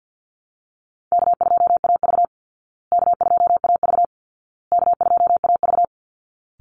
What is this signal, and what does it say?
Morse code, heard as a keyed tone, about 48 wpm, 700 Hz. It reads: K2AV K2AV K2AV
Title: Easy Call Signs Repeated 3x - 50wpm